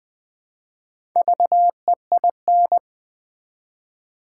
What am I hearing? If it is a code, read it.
Morse code, heard as a keyed tone, about 20 wpm, 700 Hz. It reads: VEIN
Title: Build A Word - 1 Letter At A Time - 4-Letter Words - 20wpm